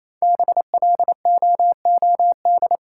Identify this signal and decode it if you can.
Morse code, heard as a keyed tone, about 28 wpm, 700 Hz. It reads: BLOOD
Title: Sentences from Top 1000 Words - Review Part 2 - New Word Drill - 28wpm